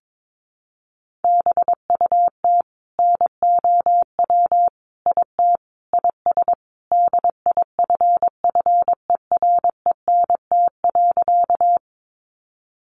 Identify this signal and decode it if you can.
Morse code, heard as a keyed tone, about 22 wpm, 700 Hz. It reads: BUT NOW IT IS DIFFERENT.
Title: Sentences from Top 600 Words 22wpm